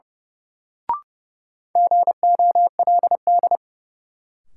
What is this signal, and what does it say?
Morse code, heard as a keyed tone, about 31 wpm, 700 Hz. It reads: GOLD
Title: Sentences from Top 1000 Words - Review Part 3 - New Word Drill - 30wpm